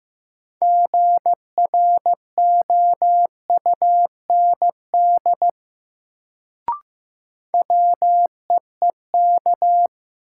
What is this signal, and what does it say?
Morse code, heard as a keyed tone, about 15 wpm, 700 Hz. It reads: GROUND WEEK
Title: Sentences from Top 500 Words - New Word Drill - 15wpm